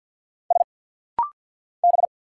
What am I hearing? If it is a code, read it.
Morse code, heard as a keyed tone, about 50 wpm, 700 Hz. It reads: S B
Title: Single Letter-Number - Rapid-Fire 50wpm